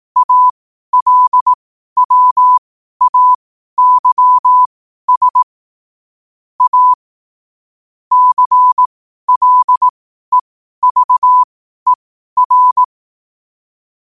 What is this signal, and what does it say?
Morse code, heard as a keyed tone, about 18 wpm, spaced as for 12 wpm, 1000 Hz. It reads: ALWAYS A CLEVER
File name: qotdfiles / 12wpm